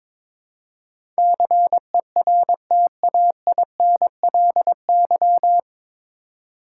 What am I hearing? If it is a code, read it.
Morse code, heard as a keyed tone, about 22 wpm, 700 Hz. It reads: CERTAINLY
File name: 9 Letter Words - First 3 Letters Given - 22wpm